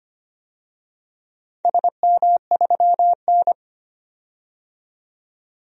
Morse code, heard as a keyed tone, about 25 wpm, 700 Hz. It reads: SM3N
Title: DXCC-100 Call Signs - 4 Characters - 25wpm